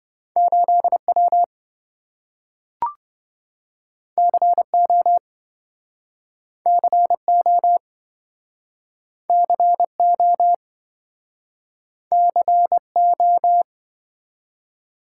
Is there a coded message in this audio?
Yes: Morse code from 0.4 to 13.6 s, heard as a keyed tone, about 28 wpm, 700 Hz. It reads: 8W CO CO CO CO